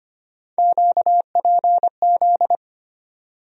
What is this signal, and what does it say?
Morse code, heard as a keyed tone, about 25 wpm, 700 Hz. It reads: QPZ
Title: USA Call Sign Postfixes - 3 Letters 25wpm